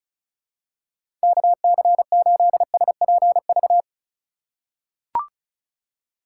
Morse code, heard as a keyed tone, about 36 wpm, 700 Hz. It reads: KC8SPV
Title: USA Call Signs - 6 Characters - 4x Character Spacing - 35wpm